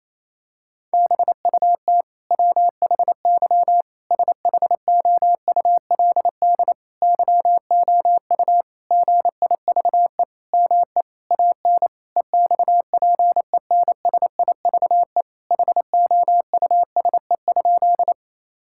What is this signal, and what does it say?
Morse code, heard as a keyed tone, about 27 wpm, 700 Hz. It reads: BUT WHY SHOULD YOU GIVE ME AN EXPENSIVE HOUSE?
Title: Sentences from Top 1000 Words - Review Part 6 - 28wpm